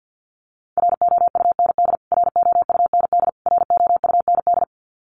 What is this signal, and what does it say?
Morse code, heard as a keyed tone, about 50 wpm, 700 Hz. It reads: ROUND ROUND ROUND
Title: Top 200 Words Repeated 50wpm